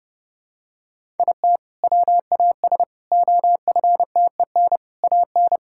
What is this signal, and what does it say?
Morse code, heard as a keyed tone, about 30 wpm, 700 Hz. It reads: IT WAS OFTEN AN
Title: Sentences from Top 1000 Words - Review Part 6 - 30wpm